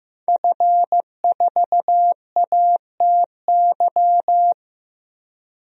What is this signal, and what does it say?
Morse code, heard as a keyed tone, about 15 wpm, 700 Hz. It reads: F4ATY
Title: DXCC-100 Call Signs - 5 Characters - 15wpm